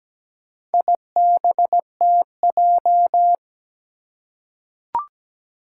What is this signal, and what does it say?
Morse code, heard as a keyed tone, about 17 wpm, 700 Hz. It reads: IBTJ
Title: Four Challenging Letter-Number Combinations 17wpm